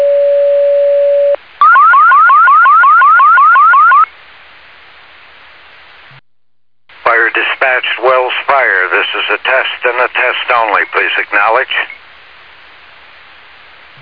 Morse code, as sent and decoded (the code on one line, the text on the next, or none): none